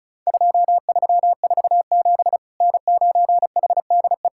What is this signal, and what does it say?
Morse code, heard as a keyed tone, about 34 wpm, 700 Hz. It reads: 2347 N9HDE
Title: Worked All Europe DX Contest - QTC Exchange - 35wpm